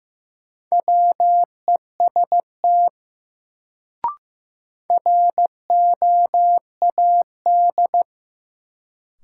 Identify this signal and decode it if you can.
Morse code, heard as a keyed tone, about 15 wpm, 700 Hz. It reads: WEST ROAD
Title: Sets of 5 Words - New Word Drill - 15wpm